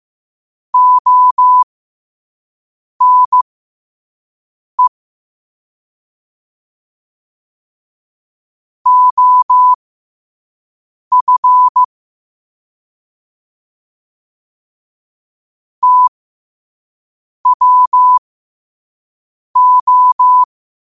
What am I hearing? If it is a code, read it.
Morse code, heard as a keyed tone, about 15 wpm, spaced as for 5 wpm, 1000 Hz. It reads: ONE OF TWO
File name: qotdfiles / 5wpm